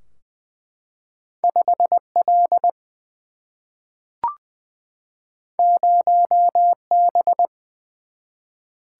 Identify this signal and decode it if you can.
Morse code, heard as a keyed tone, about 20 wpm, 700 Hz. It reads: 5L 0B
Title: Single-Letter Single-Number 20wpm